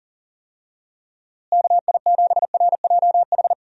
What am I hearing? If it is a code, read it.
Morse code, heard as a keyed tone, about 40 wpm, 700 Hz. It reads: KI7RJH
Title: USA Call Signs - 6 Characters - Repeated 3x - 40wpm